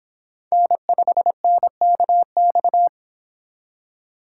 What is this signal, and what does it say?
Morse code, heard as a keyed tone, about 30 wpm, 700 Hz. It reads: N5NKX